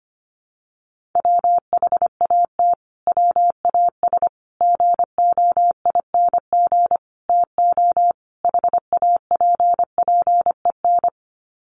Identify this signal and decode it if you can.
Morse code, heard as a keyed tone, about 25 wpm, 700 Hz. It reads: WHAT WAS GOING TO HAPPEN
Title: Sets of 5 Words Encore 25wpm